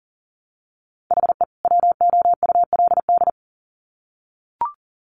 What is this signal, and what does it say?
Morse code, heard as a keyed tone, about 40 wpm, 700 Hz. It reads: HE WOULD
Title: Sets of 2 Words 40wpm